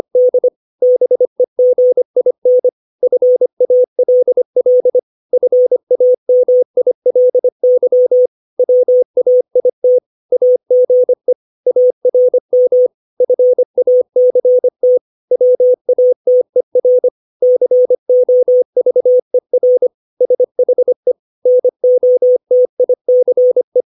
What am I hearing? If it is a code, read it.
Morse code, heard as a keyed tone, about 25 wpm, 500 Hz. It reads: D BEGIN FALL FAMILY WAIT AGE ARM FACT WATER COVER SHE NOTICE
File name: 632 words / 632 Words 25WPM